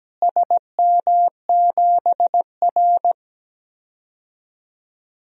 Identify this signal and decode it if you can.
Morse code, heard as a keyed tone, about 17 wpm, 700 Hz. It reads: SM7R
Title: DXCC-100 Call Signs - 4 Characters - Repeated 2x - 17wpm